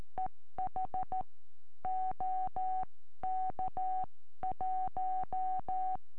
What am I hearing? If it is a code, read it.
Morse code, heard as a keyed tone, about 13 wpm, 750 Hz. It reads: EH OK1
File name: eme / sm4ive 70cm